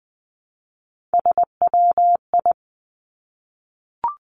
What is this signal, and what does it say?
Morse code, heard as a keyed tone, about 20 wpm, 700 Hz. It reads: SWI
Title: Three Challenging Letter-Number Combinations 20wpm